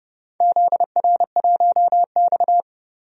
Morse code, heard as a keyed tone, about 32 wpm, 700 Hz. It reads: ZR1X
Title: DXCC-100 Call Signs - 4 Characters - Repeated 3x - 30wpm